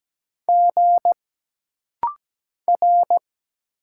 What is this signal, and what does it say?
Morse code, heard as a keyed tone, about 17 wpm, 700 Hz. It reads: G R
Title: Single Letters - Rapid-Fire 17wpm